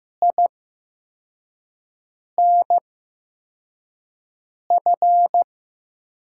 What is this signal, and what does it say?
Morse code, heard as a keyed tone, about 15 wpm, spaced as for 4 wpm, 700 Hz. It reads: INF
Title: USA Call Signs - 5 Characters - 8x Character Spacing - 15wpm